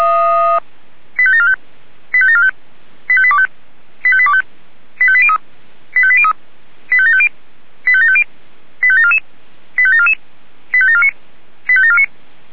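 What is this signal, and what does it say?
Five-tone selective call (ZVEI), heard as quick runs of five tones, tones of about 70 ms. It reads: 85745 85745 85825 85825 85803 85803 85750 85750 85744 85744 85749 85749
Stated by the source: ZVEI five-tone call received by SDR.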